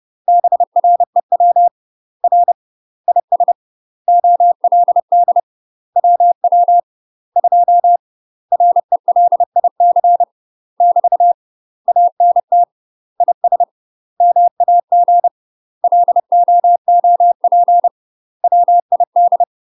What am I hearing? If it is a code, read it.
Morse code, heard as a keyed tone, about 30 wpm, 700 Hz. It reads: BREW R IS OLD WW 2 RELIC <BT> ANT IS MAG LOOP WID
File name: UK MASTER QSOs TRAINING / 30WPM / UK MASTER QSOs 0009